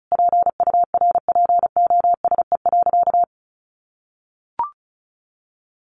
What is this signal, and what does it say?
Morse code, heard as a keyed tone, about 34 wpm, 700 Hz. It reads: PURPOSE.
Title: Sentences from Top 1500 Words 35wpm